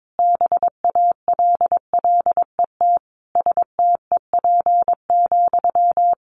Morse code, heard as a keyed tone, about 22 wpm, 700 Hz. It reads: BALLET STEP,
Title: A Sound Of Thunder 22wpm